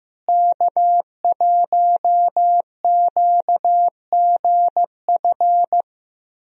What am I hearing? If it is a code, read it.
Morse code, heard as a keyed tone, about 16 wpm, 700 Hz. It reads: K1QGF
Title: USA Call Signs - 5 Characters - Repeated 3x - 15wpm